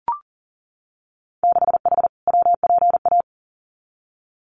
Morse code, heard as a keyed tone, about 40 wpm, 700 Hz. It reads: 6H WPA
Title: Winter Field Day Exchanges Encore 40wpm